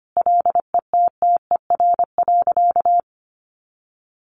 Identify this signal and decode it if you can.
Morse code, heard as a keyed tone, about 25 wpm, 700 Hz. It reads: LETTER.
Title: A Sound Of Thunder 25wpm